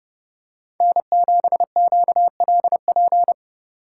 Morse code, heard as a keyed tone, about 27 wpm, 700 Hz. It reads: N7QLP